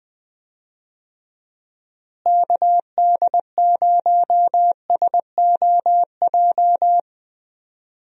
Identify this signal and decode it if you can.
Morse code, heard as a keyed tone, about 20 wpm, 700 Hz. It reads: KD0SOJ
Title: USA Call Signs - 6 Characters - Repeated 2x - 20wpm